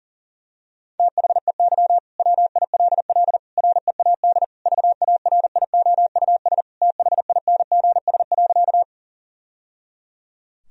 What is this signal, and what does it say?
Morse code, heard as a keyed tone, about 40 wpm, 700 Hz. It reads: THEY WILL READ VARIOUS THINGS.
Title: Sentences from Top 1000 Words - Review Part 4 - 40wpm